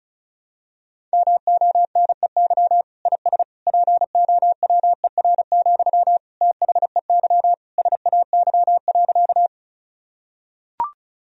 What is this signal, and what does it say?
Morse code, heard as a keyed tone, about 36 wpm, 700 Hz. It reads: MONEY IS POWER, THEY SAY.
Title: Sentences from Top 1000 Words - Review Part 2 - 35wpm